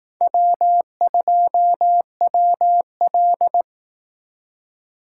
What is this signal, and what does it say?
Morse code, heard as a keyed tone, about 18 wpm, 700 Hz. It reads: W2WL
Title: Speed Racing - USA 1x2 Call Signs 15wpm